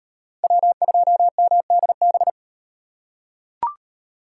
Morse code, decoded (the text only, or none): W2MDB